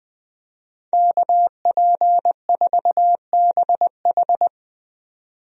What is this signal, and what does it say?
Morse code, heard as a keyed tone, about 20 wpm, 700 Hz. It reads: KP4BH
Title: DXCC-100 Call Signs - 5 Characters - 20wpm